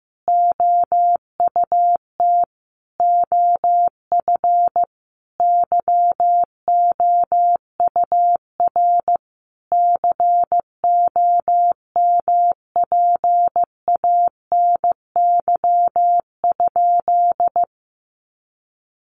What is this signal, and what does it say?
Morse code, heard as a keyed tone, about 15 wpm, 700 Hz. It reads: OUT OF YOUR COMPANY?
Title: Sentences from Top 600 Words 15wpm